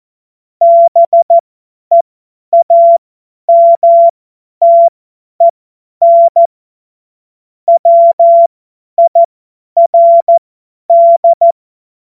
Morse code, heard as a keyed tone, about 14 wpm, spaced as for 10 wpm, 680 Hz. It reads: BEAMTEN WIRD
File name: Witz-16-September-2025-50-F70